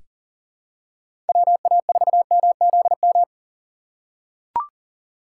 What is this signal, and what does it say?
Morse code, heard as a keyed tone, about 39 wpm, 700 Hz. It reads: WA4MZM
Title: USA Call Signs - 6 Characters - 3x Character Spacing - 40wpm